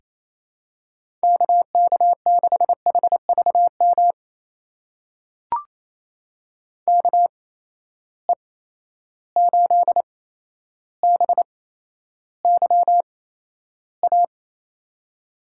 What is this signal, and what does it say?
Morse code, heard as a keyed tone, about 28 wpm, spaced as for 12 wpm, 700 Hz. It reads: KK6HVM KE8BYA
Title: USA Call Signs - 6 Characters - 8x Character Spacing - 28wpm